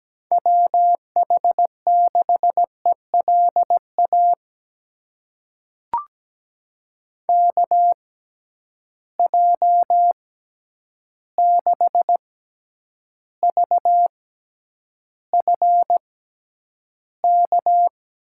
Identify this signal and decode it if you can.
Morse code, heard as a keyed tone, about 17 wpm, spaced as for 9 wpm, 700 Hz. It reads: WH6ELA KJ6VFK
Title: USA Call Signs - 6 Characters - 6x Character Spacing - 17wpm